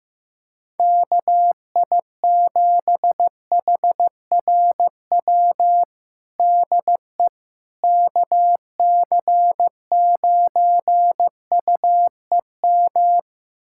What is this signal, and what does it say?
Morse code, heard as a keyed tone, about 15 wpm, 700 Hz. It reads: KI7HRW DE KC9UEM
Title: USA Callsign DE Callsign - 6 Characters - 15wpm